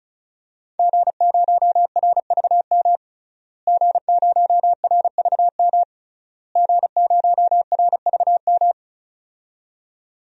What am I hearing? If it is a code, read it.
Morse code, heard as a keyed tone, about 34 wpm, 700 Hz. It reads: G0RVM G0RVM G0RVM
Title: DXCC-100 Call Signs - 5 Characters - Repeated 3x - 35wpm